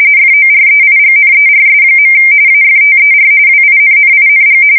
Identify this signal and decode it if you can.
RTTY, about 45.45 baud. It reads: FOREIGNERS. AFTER WINNING
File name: fsk170 / usnavy1